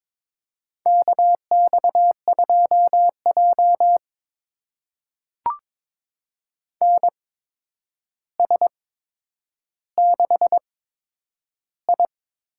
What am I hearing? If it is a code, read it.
Morse code, heard as a keyed tone, about 21 wpm, spaced as for 9 wpm, 700 Hz. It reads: KX2J NS6I
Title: USA Call Signs - 4 Characters - 8x Character Spacing - 22wpm